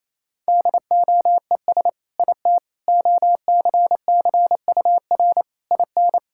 Morse code, heard as a keyed tone, about 28 wpm, 700 Hz. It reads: DOES IT OCCUR IN
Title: Sentences from Top 1000 Words - Review Part 6 - 28wpm